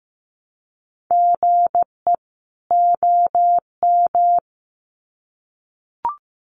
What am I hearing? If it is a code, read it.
Morse code, heard as a keyed tone, about 15 wpm, 700 Hz. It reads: GE OM
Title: Common QSO Phrases 15wpm